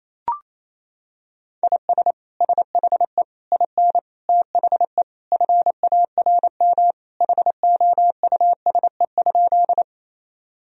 Morse code, heard as a keyed tone, about 28 wpm, 700 Hz. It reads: IS SHE IN THE FARM HOUSE?
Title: Sentences from Top 1000 Words - Review Part 2 - 28wpm